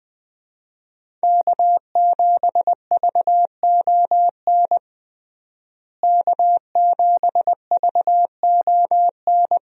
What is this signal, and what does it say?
Morse code, heard as a keyed tone, about 20 wpm, 700 Hz. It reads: K7VON K7VON
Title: USA Call Signs - 5 Characters - Repeated 2x - 20wpm